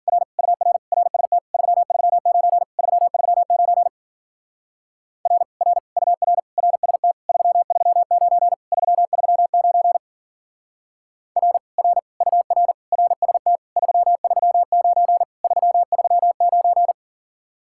Morse code, heard as a keyed tone, about 47 wpm, 700 Hz. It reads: R UR RST 339 339 R R UR RST 339 339 R R UR RST 339 339
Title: Speed-Racing Common QSO Phrases - 40wpm